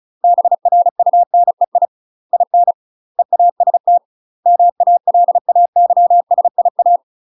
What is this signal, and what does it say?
Morse code, heard as a keyed tone, about 34 wpm, 700 Hz. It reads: BRUNEI IN EAST MALAYSIA
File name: UK MASTER QSOs TRAINING/35WPM